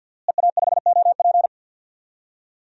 Morse code, heard as a keyed tone, about 50 wpm, 700 Hz. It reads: EA5OP
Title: DXCC-100 Call Signs - 5 Characters - Repeated 2x - 50wpm